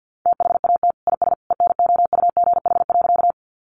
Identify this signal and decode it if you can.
Morse code, heard as a keyed tone, about 50 wpm, 700 Hz. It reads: THAT IS ENOUGH.
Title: Sentences from Top 400 Words 50wpm